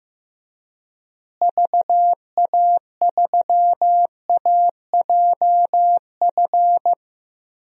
Morse code, heard as a keyed tone, about 15 wpm, 700 Hz. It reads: VA3AJF
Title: DXCC-100 Call Signs - 6 Characters - Repeated 3x - 15wpm